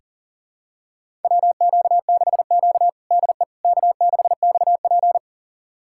Morse code, heard as a keyed tone, about 40 wpm, 700 Hz. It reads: WQ6Q DE K6XP
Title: USA Callsign DE Callsign - 4 Characters - 40wpm